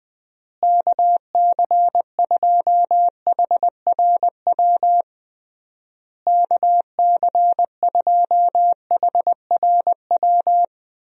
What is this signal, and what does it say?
Morse code, heard as a keyed tone, about 20 wpm, 700 Hz. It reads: KC2HRW KC2HRW
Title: USA Call Signs - 6 Characters - Repeated 3x - 20wpm